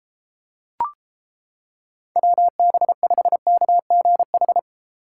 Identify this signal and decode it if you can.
Morse code, heard as a keyed tone, about 31 wpm, 700 Hz. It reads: WB5KGH